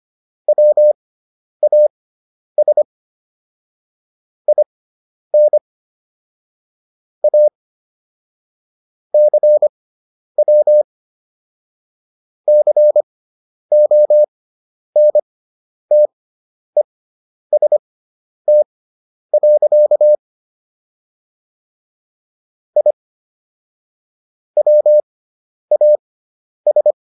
Morse code, heard as a keyed tone, about 25 wpm, spaced as for 10 wpm, 600 Hz. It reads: WAS IN A CW CONTEST. I WAS